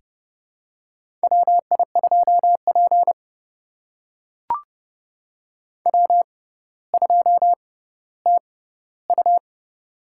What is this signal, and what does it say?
Morse code, heard as a keyed tone, about 30 wpm, spaced as for 15 wpm, 700 Hz. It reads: WI2P W2TU
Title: USA Call Signs - 4 Characters - 6x Character Spacing - 30wpm